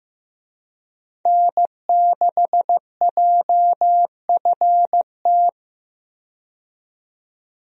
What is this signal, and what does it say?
Morse code, heard as a keyed tone, about 15 wpm, 700 Hz. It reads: N6JFT
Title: USA Call Signs - 5 Characters - 15wpm